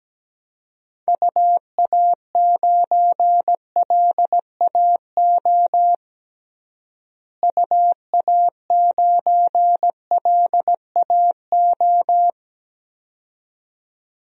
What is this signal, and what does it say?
Morse code, heard as a keyed tone, about 17 wpm, 700 Hz. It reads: UA9LAO UA9LAO
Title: DXCC-100 Call Signs - 6 Characters - Repeated 2x - 17wpm